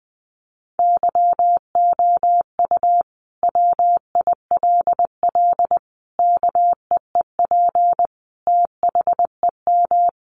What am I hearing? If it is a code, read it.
Morse code, heard as a keyed tone, about 20 wpm, 700 Hz. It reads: YOU WILL KEEP THEM
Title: Sentences from Top 1300 Words 20wpm